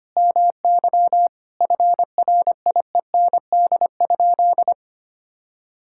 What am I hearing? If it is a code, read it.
Morse code, heard as a keyed tone, about 25 wpm, 700 Hz. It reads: MY FRIEND?
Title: Sentences from Top 1000 Words - Review Part 5 - 25wpm